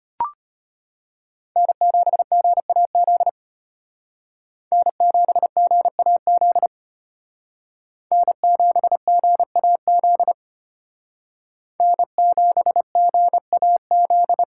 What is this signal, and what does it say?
Morse code, heard as a keyed tone, about 33 wpm, 700 Hz. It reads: N7GAZ N7GAZ N7GAZ N7GAZ